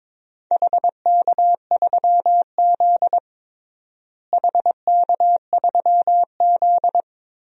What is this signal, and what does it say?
Morse code, heard as a keyed tone, about 22 wpm, 700 Hz. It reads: HK3Z HK3Z
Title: DXCC-100 Call Signs - 4 Characters - Repeated 3x - 22wpm